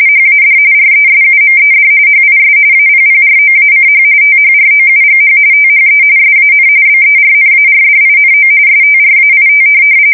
RTTY, about 45.45 baud. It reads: HAD RESUMED HIS APPROACH THROUGH THE SAN BERNARDINO STRAIT
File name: fsk170 / USS-Massachusetts